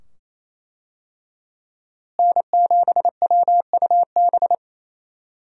Morse code, heard as a keyed tone, about 27 wpm, 700 Hz. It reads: N7WUB
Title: USA Call Signs - 5 Characters - Repeated 2x - 28wpm